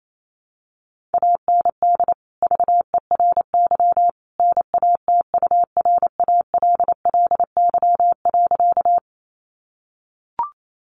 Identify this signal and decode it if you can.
Morse code, heard as a keyed tone, about 28 wpm, 700 Hz. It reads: AND VERY NATURALLY.
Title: Sentences from Top 1300 Words 28wpm